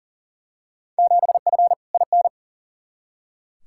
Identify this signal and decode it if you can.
Morse code, heard as a keyed tone, about 38 wpm, 700 Hz. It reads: 7F IN
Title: Field Day Exchanges Encore - 40wpm